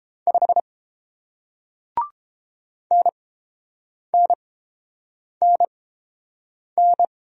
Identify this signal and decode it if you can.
Morse code, heard as a keyed tone, about 30 wpm, spaced as for 7 wpm, 700 Hz. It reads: 5 NNNN